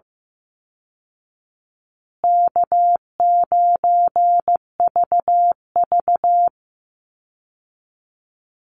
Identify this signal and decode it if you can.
Morse code, heard as a keyed tone, about 15 wpm, 700 Hz. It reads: K9VV
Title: Call Signs Repeated 3x - 15wpm